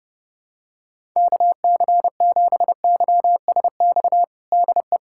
Morse code, heard as a keyed tone, about 32 wpm, 700 Hz. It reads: KC7YSX DE
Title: USA Callsign DE Callsign - 6 Characters - 30wpm